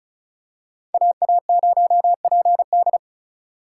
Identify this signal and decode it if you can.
Morse code, heard as a keyed tone, about 34 wpm, 700 Hz. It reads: AA0PD
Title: USA Call Signs - 6x Character Spacing - 35wpm